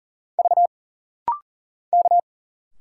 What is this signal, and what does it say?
Morse code, heard as a keyed tone, about 39 wpm, 700 Hz. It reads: V K
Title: Single Letters - Rapid-Fire 40wpm